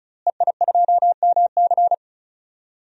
Morse code, heard as a keyed tone, about 36 wpm, 700 Hz. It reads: EI2MC
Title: DXCC-100 Call Signs - 5 Characters - Repeated 2x - 35wpm